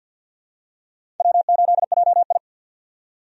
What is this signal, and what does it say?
Morse code, heard as a keyed tone, about 53 wpm, 700 Hz. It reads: W8JI
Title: USA Call Signs - 4 Characters - 50wpm